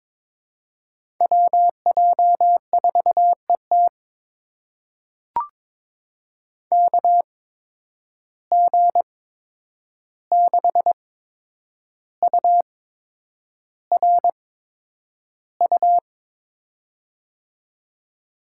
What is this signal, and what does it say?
Morse code, heard as a keyed tone, about 22 wpm, spaced as for 9 wpm, 700 Hz. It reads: WJ4ET KG6URU
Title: USA Call Signs - 8x Character Spacing - 22wpm